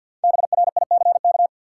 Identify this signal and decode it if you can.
Morse code, heard as a keyed tone, about 50 wpm, 700 Hz. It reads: BRICK
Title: Five Letter Words - 50wpm